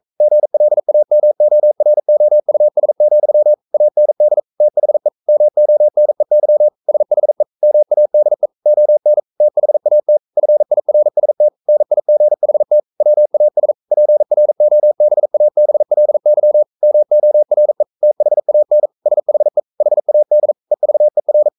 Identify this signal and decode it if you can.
Morse code, heard as a keyed tone, about 44 wpm, 600 Hz. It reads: GLAMOROUS, AND THE MONEY SHE MADE ON THAT FIRST NIGHT WAS PROBABLY MORE THAN SHE HAD EVER
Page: Romanen Never af Ken Follet på engelsk.